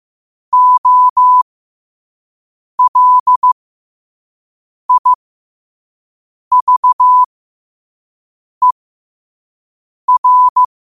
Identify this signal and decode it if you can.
Morse code, heard as a keyed tone, about 15 wpm, spaced as for 5 wpm, 1000 Hz. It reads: OLIVER
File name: qotdfiles / 5wpm